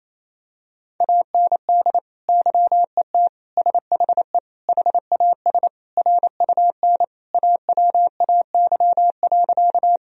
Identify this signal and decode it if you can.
Morse code, heard as a keyed tone, about 28 wpm, 700 Hz. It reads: AND YET SHE HAS RUN AWAY.
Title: Sentences from Top 1000 Words - Review Part 1 - 28wpm